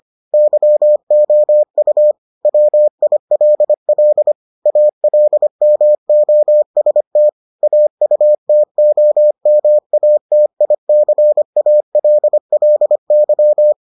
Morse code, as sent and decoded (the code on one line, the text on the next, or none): -.-- --- ..- / .-- .. .-.. .-.. / .- .-.. -- --- ... - / .- ..- - --- -- .- - .. -.-. .- .-.. .-.. -.--
YOU WILL ALMOST AUTOMATICALLY